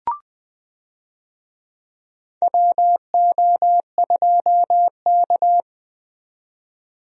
Morse code, heard as a keyed tone, about 20 wpm, 700 Hz. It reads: WO2K
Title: USA Call Signs - 4 Characters - 20wpm